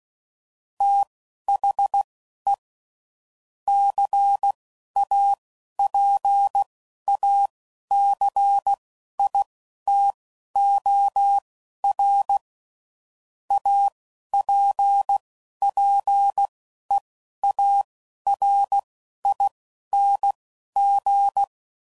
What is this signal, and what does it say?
Morse code, heard as a keyed tone, about 16 wpm, spaced as for 11 wpm, 800 Hz. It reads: THE CAPACITOR APPEARING
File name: CW Code Groups @10WPM / Technical Subjects / XC